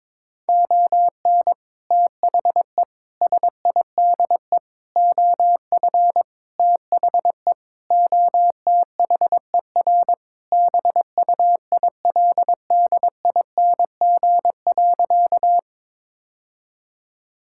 Morse code, heard as a keyed tone, about 22 wpm, 700 Hz. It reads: ON THE SIDE OF THE OTHER BUILDING.
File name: practice / 22wpm / Sentences with 3-Word Phrases - 22wpm